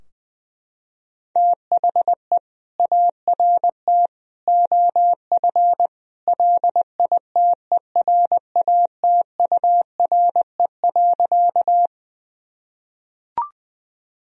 Morse code, heard as a keyed tone, about 20 wpm, 700 Hz. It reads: THE ART OF LITERATURE.
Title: Sentences from Top 1000 Words - Review Part 3 - 20wpm